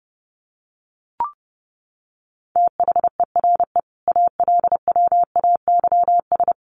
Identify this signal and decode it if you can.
Morse code, heard as a keyed tone, about 30 wpm, 700 Hz. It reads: THERE ALWAYS
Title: Sentences from Top 1100 Words 30wpm